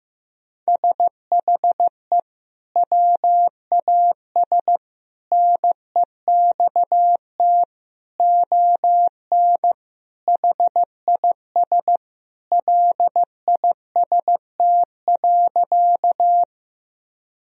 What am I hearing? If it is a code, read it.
Morse code, heard as a keyed tone, about 15 wpm, 700 Hz. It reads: SHE WAS NEXT ON HIS LIST.
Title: Sentences from Top 1000 Words - Review Part 2 - 15wpm